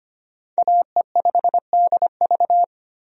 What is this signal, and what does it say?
Morse code, heard as a keyed tone, about 25 wpm, 700 Hz. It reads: AE5DV
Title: USA Call Signs - 5 Characters - Repeated 2x - 25wpm